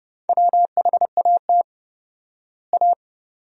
Morse code, heard as a keyed tone, about 32 wpm, 700 Hz. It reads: WHAT A
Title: 4x Word-Spacing - Sentences from Top 500 Words - 30wpm